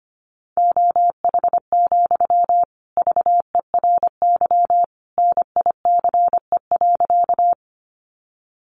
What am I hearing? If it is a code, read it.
Morse code, heard as a keyed tone, about 25 wpm, 700 Hz. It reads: OH, VERY NICE.
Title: Sentences from Top 1100 Words 25wpm